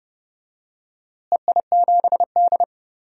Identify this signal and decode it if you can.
Morse code, heard as a keyed tone, about 31 wpm, 700 Hz. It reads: EI7D
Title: DXCC-100 Call Signs - 4 Characters - Repeated 2x - 30wpm